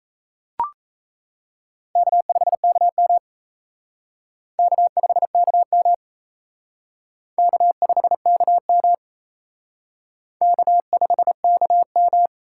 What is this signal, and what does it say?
Morse code, heard as a keyed tone, about 34 wpm, 700 Hz. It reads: K5KM K5KM K5KM K5KM
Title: Speed Racing - USA 1x2 Call Signs 28wpm